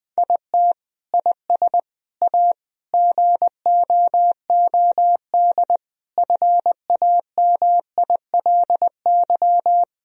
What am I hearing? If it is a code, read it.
Morse code, heard as a keyed tone, about 21 wpm, 700 Hz. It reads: IT IS A GOOD FAMILY
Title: Sentences from Top 1000 Words - Review Part 2 - 20wpm